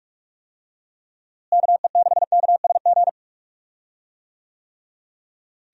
Morse code, heard as a keyed tone, about 47 wpm, 700 Hz. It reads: KE6KSG
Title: USA Call Signs - 6 Characters - Repeated 2x - 45wpm